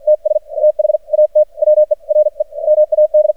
Morse code, heard as a keyed tone, about 54 wpm, spaced as for 43 wpm, 610 Hz. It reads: TS THAT WERE MAD
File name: archived / CRSnet